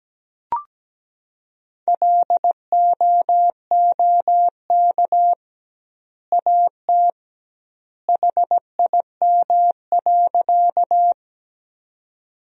Morse code, heard as a keyed tone, about 17 wpm, 700 Hz. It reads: LOOK AT HIM.
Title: 2x Word-Spacing - Sentences from Top 100 Words - 17wpm